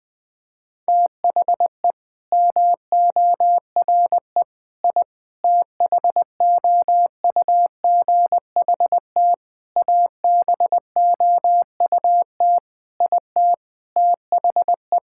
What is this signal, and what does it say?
Morse code, heard as a keyed tone, about 20 wpm, 700 Hz. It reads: THE MORE I THOUGHT ABOUT IT THE
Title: Sentences with 8-Word Phrases - 20wpm